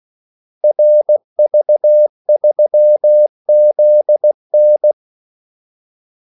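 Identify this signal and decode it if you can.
Morse code, heard as a keyed tone, about 16 wpm, 600 Hz. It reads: RV3ZN
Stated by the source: Morse call sign practice at 16 wpm.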